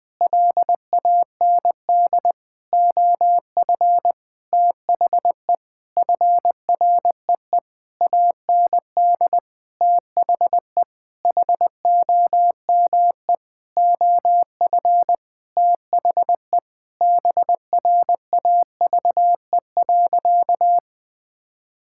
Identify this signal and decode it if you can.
Morse code, heard as a keyed tone, about 20 wpm, 700 Hz. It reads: LAND OF THE FREE AND THE HOME OF THE BRAVE.
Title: Sentences with 8-Word Phrases - Spoken Beginning - 20wpm